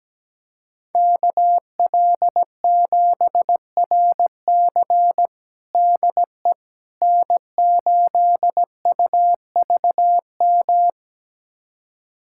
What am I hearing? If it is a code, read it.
Morse code, heard as a keyed tone, about 17 wpm, 700 Hz. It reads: KL7RC DE N8UVM
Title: USA Callsign DE Callsign - 5 Characters - 17wpm